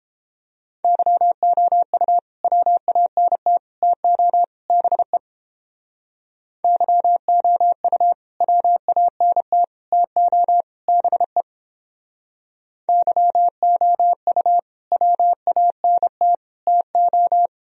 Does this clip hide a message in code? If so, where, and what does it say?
Yes: Morse code from 0.8 to 17.5 s, heard as a keyed tone, about 31 wpm, 700 Hz. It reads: YOU WANT TO BE YOU WANT TO BE YOU WANT TO